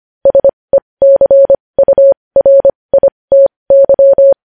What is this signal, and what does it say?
Morse code, heard as a keyed tone, about 25 wpm, spaced as for 20 wpm, 550 Hz. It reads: SECURITY
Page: News headlines in morse code at 20 wpm